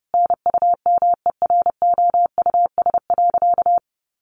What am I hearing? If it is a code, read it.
Morse code, heard as a keyed tone, about 30 wpm, 700 Hz. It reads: NUMEROUS.
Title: Sentences from Top 1400 Words 30wpm